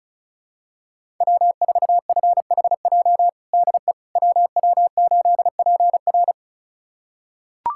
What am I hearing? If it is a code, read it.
Morse code, heard as a keyed tone, about 34 wpm, 700 Hz. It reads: W4FHJ DE WW8PR
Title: USA Callsign DE Callsign - 5 Characters - 35wpm